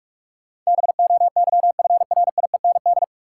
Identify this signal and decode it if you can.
Morse code, heard as a keyed tone, about 47 wpm, 700 Hz. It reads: BOYFRIEND
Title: Sentences from Top 1000 Words - Review Part 5 - New Word Drill - 45wpm